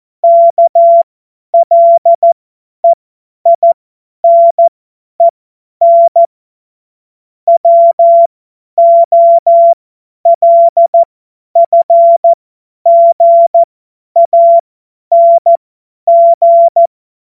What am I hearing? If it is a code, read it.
Morse code, heard as a keyed tone, about 14 wpm, spaced as for 10 wpm, 680 Hz. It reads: KLEINEN WOLFGANG
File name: Morsekurs / Witze / Witz 18. April 2023 50 F70